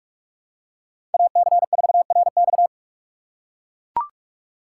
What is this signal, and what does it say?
Morse code, heard as a keyed tone, about 41 wpm, 700 Hz. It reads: AC4RX
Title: USA Call Signs - 5 Characters - 3x Character Spacing - 45wpm